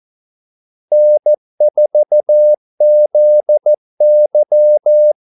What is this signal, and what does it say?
Morse code, heard as a keyed tone, about 14 wpm, 600 Hz. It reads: N4ZY
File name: Callsigns-14wpm